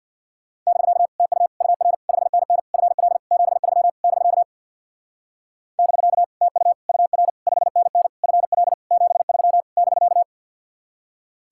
Morse code, heard as a keyed tone, about 59 wpm, 700 Hz. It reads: <BK> TU UR 5NN FL 73 <BK> <BK> TU UR 5NN FL 73 <BK>
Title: Speed-Racing Common QSO Phrases - 50wpm